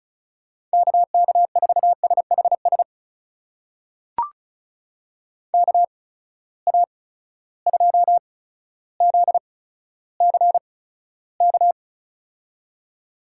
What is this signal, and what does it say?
Morse code, heard as a keyed tone, about 36 wpm, spaced as for 15 wpm, 700 Hz. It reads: KK4SHS KA2ZCK
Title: USA Call Signs - 6 Characters - 8x Character Spacing - 35wpm